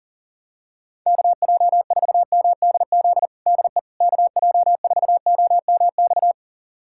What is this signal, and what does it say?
Morse code, heard as a keyed tone, about 40 wpm, 700 Hz. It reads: KJ4MDZ DE KJ4OMX
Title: USA Callsign DE Callsign - 6 Characters - 40wpm